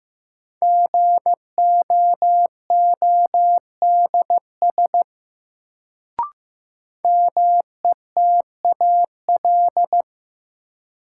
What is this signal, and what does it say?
Morse code, heard as a keyed tone, about 15 wpm, 700 Hz. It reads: GOODS METAL
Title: Sentences from Top 1000 Words - Review Part 6 - New Word Drill - 15wpm